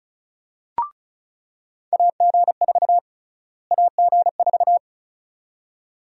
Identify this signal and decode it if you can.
Morse code, heard as a keyed tone, about 36 wpm, 700 Hz. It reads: AG4 AG4
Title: USA Call Sign Prefixes - Repeated 2x - 35wpm